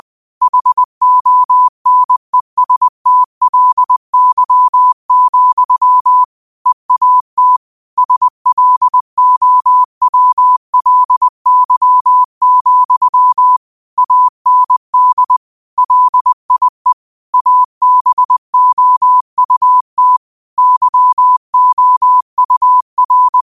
Morse code, heard as a keyed tone, about 20 wpm, 1000 Hz. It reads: HONESTLY, EAT SLOWLY, AND LIE ABOUT YOUR
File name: qotdfiles / 20wpm